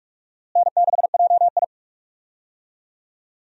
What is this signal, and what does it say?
Morse code, heard as a keyed tone, about 46 wpm, 700 Hz. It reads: N6JI
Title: USA Call Signs - 4 Characters - Repeated 3x - 45wpm